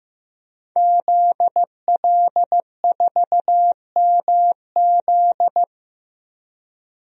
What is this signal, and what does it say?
Morse code, heard as a keyed tone, about 15 wpm, 700 Hz. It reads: ZL4MZ
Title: DXCC-100 Call Signs - 5 Characters - Repeated 2x - 15wpm